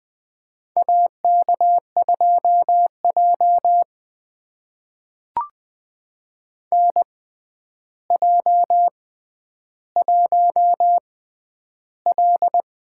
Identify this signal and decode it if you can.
Morse code, heard as a keyed tone, about 20 wpm, spaced as for 10 wpm, 700 Hz. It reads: AK2J NJ1L
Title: USA Call Signs - 4 Characters - 6x Character Spacing - 20wpm